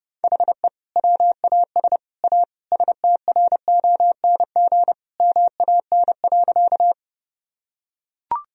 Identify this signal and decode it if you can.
Morse code, heard as a keyed tone, about 30 wpm, 700 Hz. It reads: HE WAS A STRONG MAN.
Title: Sentences from Top 1000 Words - Review Part 3 - 30wpm